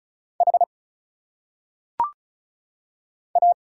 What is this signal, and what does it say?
Morse code, heard as a keyed tone, about 34 wpm, 700 Hz. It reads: H A
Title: Single Letters - Phonetic Pronunciation 35wpm